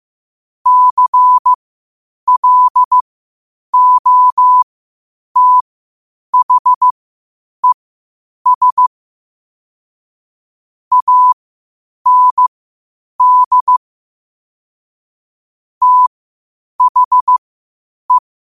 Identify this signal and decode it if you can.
Morse code, heard as a keyed tone, about 15 wpm, spaced as for 8 wpm, 1000 Hz. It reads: CLOTHES AND THE
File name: qotdfiles / 8wpm